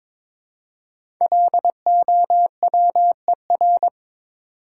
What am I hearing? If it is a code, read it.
Morse code, heard as a keyed tone, about 22 wpm, 700 Hz. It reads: LOWER
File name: 5 Letter Words - First 3 Letters Given - 22wpm